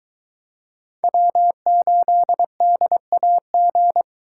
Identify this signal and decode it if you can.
Morse code, heard as a keyed tone, about 21 wpm, 700 Hz. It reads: W8DAG